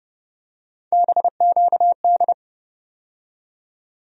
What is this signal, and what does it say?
Morse code, heard as a keyed tone, about 31 wpm, 700 Hz. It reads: BQD
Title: USA Call Sign Postfixes - 3 Letters 30wpm